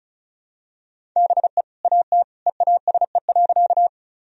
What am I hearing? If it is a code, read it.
Morse code, heard as a keyed tone, about 36 wpm, 700 Hz. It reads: BE AT EASE.
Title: Sentences from Top 1000 Words - Review Part 1 - 35wpm